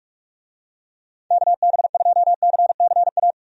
Morse code, heard as a keyed tone, about 46 wpm, 700 Hz. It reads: KB2CCA
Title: USA Call Signs - 6 Characters - 45wpm